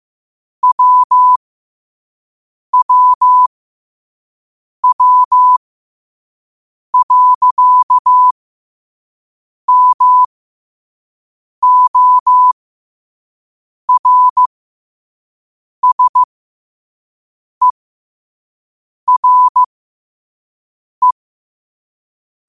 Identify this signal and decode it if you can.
Morse code, heard as a keyed tone, about 15 wpm, spaced as for 5 wpm, 1000 Hz. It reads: WWW.MORSERE